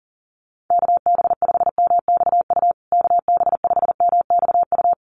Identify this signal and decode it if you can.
Morse code, heard as a keyed tone, about 40 wpm, 700 Hz. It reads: KB5MXU KB5MXU
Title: USA Call Signs Repeated 2x 40wpm